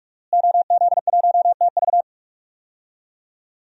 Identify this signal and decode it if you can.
Morse code, heard as a keyed tone, about 46 wpm, 700 Hz. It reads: OZ1TV
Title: DXCC-100 Call Signs - 5 Characters - Repeated 3x - 45wpm